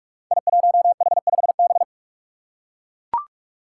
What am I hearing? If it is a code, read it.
Morse code, heard as a keyed tone, about 42 wpm, 700 Hz. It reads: I1H5B
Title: Five Challenging Letter-Number Combinations 45wpm